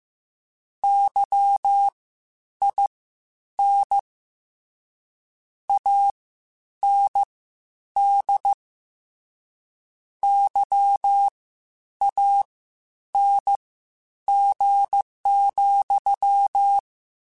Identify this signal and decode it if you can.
Morse code, heard as a keyed tone, about 15 wpm, spaced as for 9 wpm, 800 Hz. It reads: YIN AND YANG,
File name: Misc / Misc-mp3s / WordPairs sp @13